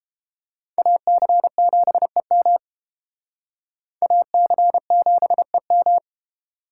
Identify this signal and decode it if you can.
Morse code, heard as a keyed tone, about 29 wpm, 700 Hz. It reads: AC7EM AC7EM